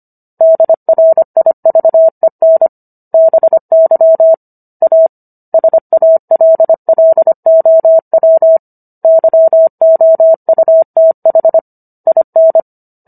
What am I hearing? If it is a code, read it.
Morse code, heard as a keyed tone, about 25 wpm, 650 Hz. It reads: DRIVEN BY A SALLOW YOUTH IN